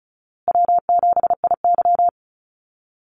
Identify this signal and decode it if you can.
Morse code, heard as a keyed tone, about 34 wpm, 700 Hz. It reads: W7IY
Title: Easy Call Signs Repeated 3x - 35wpm